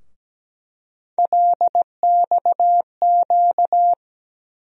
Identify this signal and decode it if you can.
Morse code, heard as a keyed tone, about 17 wpm, 700 Hz. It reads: LXQ
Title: USA Call Sign Postfixes - 3 Letters 17wpm